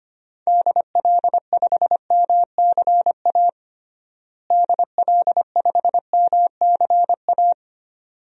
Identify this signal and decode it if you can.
Morse code, heard as a keyed tone, about 25 wpm, 700 Hz. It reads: DL5MCA DL5MCA
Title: DXCC-100 Call Signs - 6 Characters - Repeated 2x - 25wpm